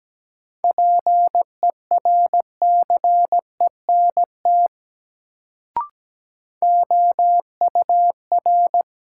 Morse code, heard as a keyed tone, about 17 wpm, 700 Hz. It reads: PERCENT OUR
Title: Sets of 2 Words - New Word Drill - 17wpm